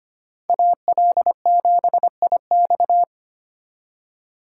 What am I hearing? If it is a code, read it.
Morse code, heard as a keyed tone, about 32 wpm, 700 Hz. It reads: AL7IX